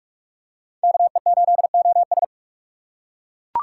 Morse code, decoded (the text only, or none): KE8OS